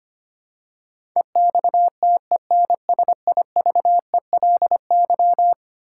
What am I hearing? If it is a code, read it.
Morse code, heard as a keyed tone, about 25 wpm, 700 Hz. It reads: EXTENSIVELY
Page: morse-code-ninja-practice-25wpm